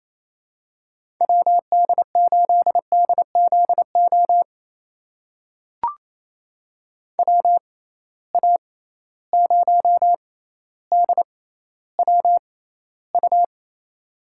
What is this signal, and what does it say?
Morse code, heard as a keyed tone, about 28 wpm, spaced as for 14 wpm, 700 Hz. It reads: WD8DZO WA0DWU
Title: USA Call Signs - 6 Characters - 6x Character Spacing - 28wpm